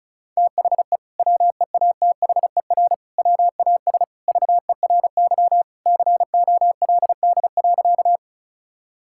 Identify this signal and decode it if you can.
Morse code, heard as a keyed tone, about 34 wpm, 700 Hz. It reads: THE WEATHER WAS VERY COLD.
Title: Sentences from Top 1000 Words - Review Part 5 - 35wpm